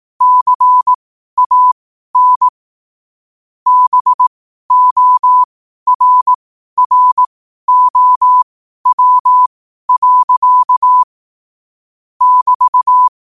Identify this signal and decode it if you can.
Morse code, heard as a keyed tone, about 18 wpm, spaced as for 12 wpm, 1000 Hz. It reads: CAN BORROW. <BT>
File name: qotdfiles / 12wpm